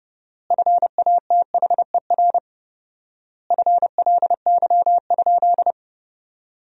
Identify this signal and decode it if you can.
Morse code, heard as a keyed tone, about 30 wpm, 700 Hz. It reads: FATHER FLY?
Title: 4x Word-Spacing - Sentences from Top 500 Words - 30wpm